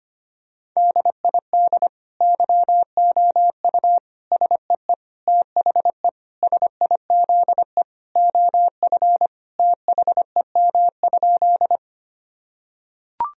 Morse code, heard as a keyed tone, about 25 wpm, 700 Hz. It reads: DID YOU SEE THE SIZE OF THEM?
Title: Sentences from Top 1000 Words - Review Part 2 - 25wpm